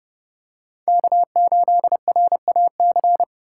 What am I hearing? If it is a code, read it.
Morse code, heard as a keyed tone, about 28 wpm, 700 Hz. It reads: K8RAC